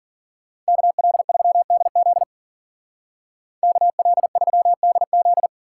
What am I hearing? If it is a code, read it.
Morse code, heard as a keyed tone, about 53 wpm, 700 Hz. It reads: KL3DZ KL3DZ